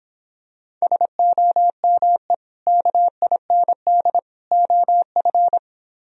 Morse code, heard as a keyed tone, about 26 wpm, 700 Hz. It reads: SOME KIND OF